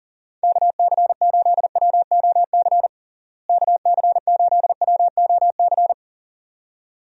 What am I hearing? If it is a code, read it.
Morse code, heard as a keyed tone, about 37 wpm, 700 Hz. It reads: KC8WOC KC8WOC
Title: USA Call Signs - 6 Characters - Repeated 2x - 40wpm